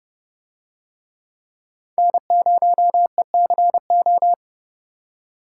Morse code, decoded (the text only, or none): N0ECO